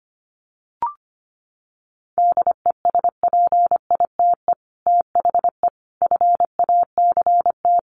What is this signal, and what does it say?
Morse code, heard as a keyed tone, about 25 wpm, 700 Hz. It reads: DESPITE THE FACT
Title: Sets of 4 Words 25wpm